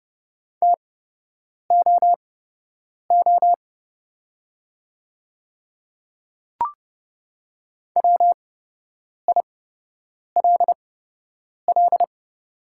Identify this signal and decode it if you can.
Morse code, heard as a keyed tone, about 29 wpm, spaced as for 8 wpm, 700 Hz. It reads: TOO WILL
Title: 8x Character Spacing - Top 300 Words - 30wpm